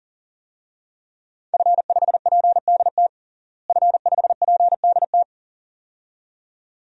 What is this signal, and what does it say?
Morse code, heard as a keyed tone, about 40 wpm, 700 Hz. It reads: F5PDT F5PDT
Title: DXCC-100 Call Signs - 5 Characters - Repeated 2x - 40wpm